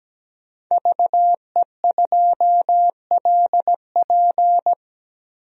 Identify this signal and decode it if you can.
Morse code, heard as a keyed tone, about 17 wpm, 700 Hz. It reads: VE2LP
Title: DXCC-100 Call Signs - 5 Characters - 17wpm